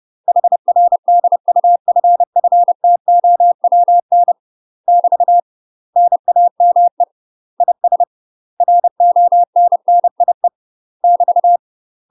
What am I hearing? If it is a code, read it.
Morse code, heard as a keyed tone, about 30 wpm, 700 Hz. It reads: HR DUFFTOWN <BT> NAME IS RONNIE <BT>
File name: UK MASTER QSOs TRAINING / 30WPM / UK MASTER QSOs 0010